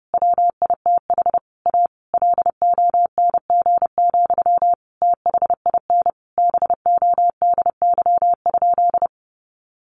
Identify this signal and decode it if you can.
Morse code, heard as a keyed tone, about 30 wpm, 700 Hz. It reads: WITH A LONG, THIN BODY?
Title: Sentences from Top 1200 Words 30wpm